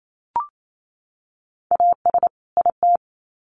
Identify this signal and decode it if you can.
Morse code, heard as a keyed tone, about 28 wpm, 700 Hz. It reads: AS IT
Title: Sets of 2 Words 28wpm